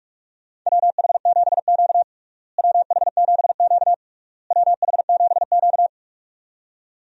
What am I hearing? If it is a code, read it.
Morse code, heard as a keyed tone, about 45 wpm, 700 Hz. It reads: WH7Q WH7Q WH7Q
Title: USA Call Signs - 4 Characters - Repeated 3x - 45wpm